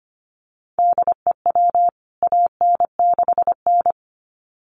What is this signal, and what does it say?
Morse code, heard as a keyed tone, about 25 wpm, 700 Hz. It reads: DEW AN6N
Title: CWT Exchange 25wpm